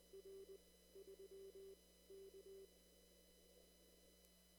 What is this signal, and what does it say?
Morse code, heard as a keyed tone, about 20 wpm, 390 Hz. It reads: R 3 K
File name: CW / kt5j wpx cw-op-k5pi